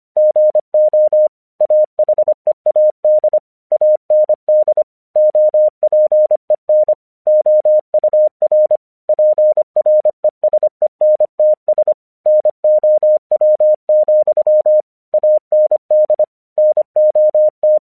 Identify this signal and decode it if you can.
Morse code, heard as a keyed tone, about 25 wpm, 610 Hz. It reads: GO AHEAD AND OPEN OUR PRESENTS NOW, AND NOT
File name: 1 English 25wpm / AQC19